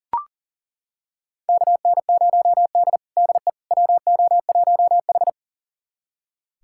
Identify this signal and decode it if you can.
Morse code, heard as a keyed tone, about 40 wpm, 700 Hz. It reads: KN0D DE WO1H
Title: USA Callsign DE Callsign - 4 Characters - 40wpm